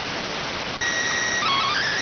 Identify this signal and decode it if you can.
SSTV (slow-scan television), PD 290.